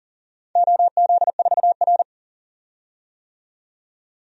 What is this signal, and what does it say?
Morse code, heard as a keyed tone, about 40 wpm, 700 Hz. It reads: OZ4R
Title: DXCC-100 Call Signs - 4 Characters - 40wpm